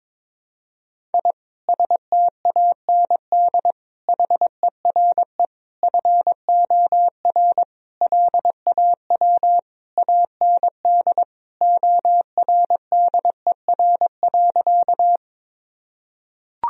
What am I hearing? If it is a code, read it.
Morse code, heard as a keyed tone, about 22 wpm, 700 Hz. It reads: I STAND HERE FOR LAW AND ORDER.
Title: Sentences from Top 1000 Words - Review Part 3 - 22wpm